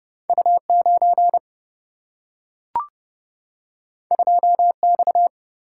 Morse code, heard as a keyed tone, about 30 wpm, 700 Hz. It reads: U 9 2 X
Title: Single-Letter Single-Number 30wpm